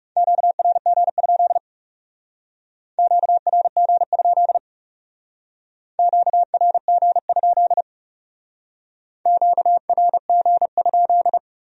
Morse code, heard as a keyed tone, about 39 wpm, 700 Hz. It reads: QRG? QRG? QRG? QRG?